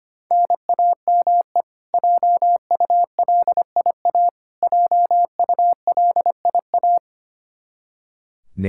Morse code, heard as a keyed tone, about 25 wpm, 700 Hz. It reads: NAME JULIA JULIA
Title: Practice QSO - Exchange 1 - Beginning - 25wpm